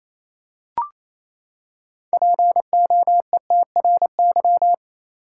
Morse code, heard as a keyed tone, about 27 wpm, 700 Hz. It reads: POETRY
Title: Six Letter Words - 28wpm